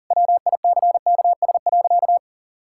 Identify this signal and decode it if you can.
Morse code, heard as a keyed tone, about 40 wpm, 700 Hz. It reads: WICKS.
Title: Pangram Challenge v2 - 40wpm